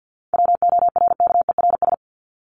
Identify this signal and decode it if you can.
Morse code, heard as a keyed tone, about 48 wpm, 700 Hz. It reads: WORKERS
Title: ES:S Suffix - Top 100 Words with Base Words 50wpm